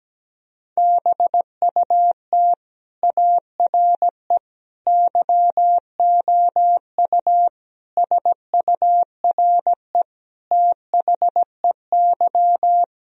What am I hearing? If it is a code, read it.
Morse code, heard as a keyed tone, about 17 wpm, 700 Hz. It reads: BUT ARE YOU SURE THEY
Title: Sentences from Top 1000 Words - Review Part 1 - 17wpm